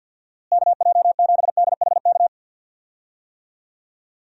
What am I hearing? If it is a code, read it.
Morse code, heard as a keyed tone, about 53 wpm, 700 Hz. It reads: KJ7DHK
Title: USA Call Signs - 6 Characters - Repeated 2x - 50wpm